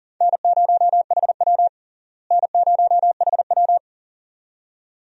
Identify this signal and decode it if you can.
Morse code, heard as a keyed tone, about 40 wpm, 700 Hz. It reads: N0HW N0HW
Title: USA Call Signs - 4 Characters - Repeated 2x - 40wpm